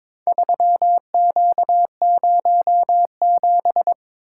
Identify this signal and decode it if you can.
Morse code, heard as a keyed tone, about 22 wpm, 700 Hz. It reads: 3Q07
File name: practice / 22wpm / Four Letter-Number Combinations - 22wpm